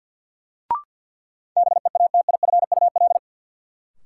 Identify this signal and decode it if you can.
Morse code, heard as a keyed tone, about 52 wpm, 700 Hz. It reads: BEATIFUL
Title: Sentences from Top 1300 Words - New Word Drill Alt - 50wpm